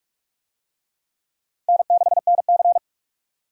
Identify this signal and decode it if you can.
Morse code, heard as a keyed tone, about 42 wpm, 700 Hz. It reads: N6NC